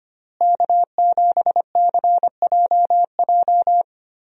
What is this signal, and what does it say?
Morse code, heard as a keyed tone, about 25 wpm, 700 Hz. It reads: K7CJJ
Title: USA Call Signs - 5 Characters - Repeated 3x - 25wpm